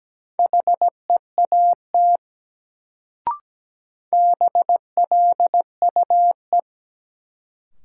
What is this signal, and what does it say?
Morse code, heard as a keyed tone, about 17 wpm, 700 Hz. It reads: HEAT BLUE
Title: Sentences from Top 1000 Words - Review Part 3 - New Word Drill - 17wpm